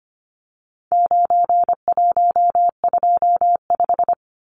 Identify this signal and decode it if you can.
Morse code, heard as a keyed tone, about 25 wpm, 700 Hz. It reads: 9125
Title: Numbers - 4-Digit 25wpm